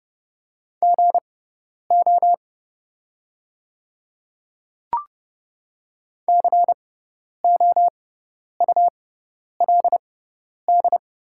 Morse code, heard as a keyed tone, about 30 wpm, spaced as for 10 wpm, 700 Hz. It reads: GO COULD
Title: 6x Character Spacing - Top 100 Words - 30wpm